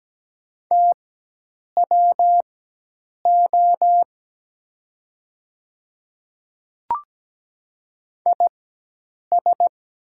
Morse code, heard as a keyed tone, about 17 wpm, spaced as for 8 wpm, 700 Hz. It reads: TWO IS
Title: 4x Character Spacing - Top 100 Words - 17wpm